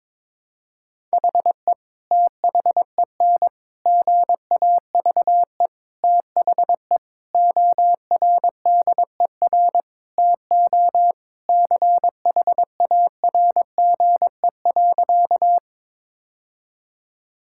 Morse code, heard as a keyed tone, about 22 wpm, 700 Hz. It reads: HE THEN GAVE THE ORDER TO CHARGE.
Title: Sentences from Top 1000 Words - Review Part 6 - 22wpm